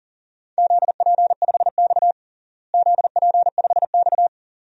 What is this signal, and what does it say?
Morse code, heard as a keyed tone, about 40 wpm, 700 Hz. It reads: ZP5X ZP5X
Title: DXCC-100 Call Signs - 4 Characters - Repeated 2x - 40wpm